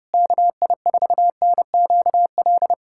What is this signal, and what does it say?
Morse code, heard as a keyed tone, about 32 wpm, 700 Hz. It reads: KI4NQL
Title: USA Callsign DE Callsign 30wpm